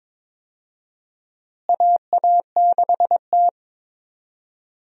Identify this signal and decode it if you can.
Morse code, heard as a keyed tone, about 21 wpm, 700 Hz. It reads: AA6T
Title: USA Call Signs - 4 Characters - Repeated 2x - 22wpm